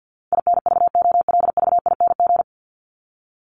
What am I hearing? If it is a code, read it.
Morse code, heard as a keyed tone, about 50 wpm, 700 Hz. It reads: INVOLVING
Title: ING Suffix - Top 100 Words with Base Words 50wpm